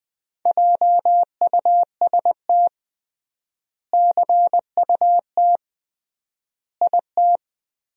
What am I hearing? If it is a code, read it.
Morse code, heard as a keyed tone, about 20 wpm, 700 Hz. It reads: JUST CUT IT
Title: 3x Word-Spacing - Sentences from Top 300 Words - 20wpm